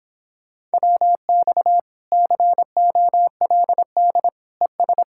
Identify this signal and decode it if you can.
Morse code, heard as a keyed tone, about 26 wpm, 700 Hz. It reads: WX COLD ES